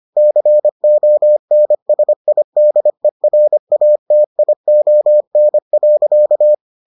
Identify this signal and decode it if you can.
Morse code, heard as a keyed tone, about 25 wpm, 600 Hz. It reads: CONSIDERATION.